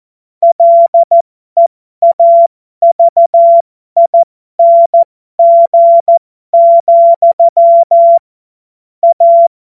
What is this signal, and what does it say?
Morse code, heard as a keyed tone, about 14 wpm, spaced as for 12 wpm, 680 Hz. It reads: LEAVING, A
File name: Morsekurs / Witze